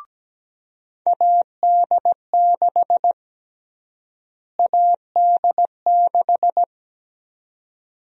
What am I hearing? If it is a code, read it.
Morse code, heard as a keyed tone, about 17 wpm, 700 Hz. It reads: AD6 AD6
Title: USA Call Sign Prefixes - Repeated 2x - 17wpm